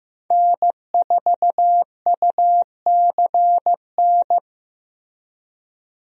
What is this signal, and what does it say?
Morse code, heard as a keyed tone, about 15 wpm, 700 Hz. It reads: N4UCN
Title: USA Call Signs - 5 Characters - Repeated 3x - 15wpm